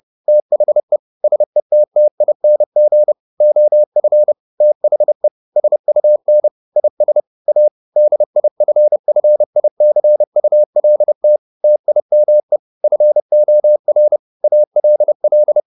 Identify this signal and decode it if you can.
Morse code, heard as a keyed tone, about 30 wpm, 600 Hz. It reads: THE SETTING OF THE SUN IS A DIFFICULT TIME FOR ALL